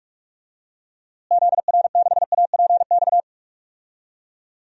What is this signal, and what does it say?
Morse code, heard as a keyed tone, about 42 wpm, 700 Hz. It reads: ZR6APX
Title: DXCC-100 Call Signs - 6 Characters - Repeated 2x - 45wpm